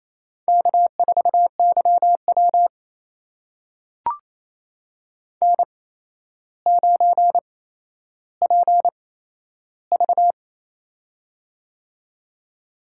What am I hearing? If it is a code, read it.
Morse code, heard as a keyed tone, about 27 wpm, spaced as for 12 wpm, 700 Hz. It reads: K4YW N9PV
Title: USA Call Signs - 4 Characters - 8x Character Spacing - 28wpm